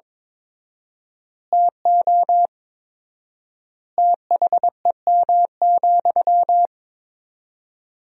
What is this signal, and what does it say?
Morse code, heard as a keyed tone, about 22 wpm, 700 Hz. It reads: TO THEM,
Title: 4x Word-Spacing - Sentences from Top 400 Words - 22wpm